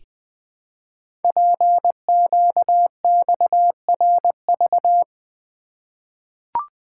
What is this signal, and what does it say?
Morse code, heard as a keyed tone, about 21 wpm, 700 Hz. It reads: PQXRV
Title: Code Groups - 5 Letters - 20wpm